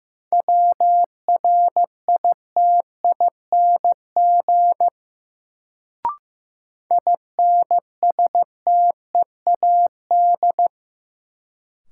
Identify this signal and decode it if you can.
Morse code, heard as a keyed tone, about 15 wpm, 700 Hz. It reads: WRITING INSTEAD
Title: Sentences from Top 800 Words - New Word Drill - 15wpm